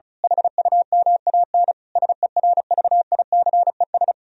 Morse code, heard as a keyed tone, about 32 wpm, 700 Hz. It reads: HUMAN SERVICES
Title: Speed-Racing - Sets of 5 Words 35wpm